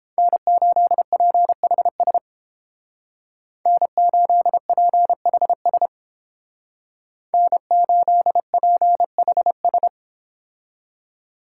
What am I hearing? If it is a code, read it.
Morse code, heard as a keyed tone, about 31 wpm, 700 Hz. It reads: N8PHS N8PHS N8PHS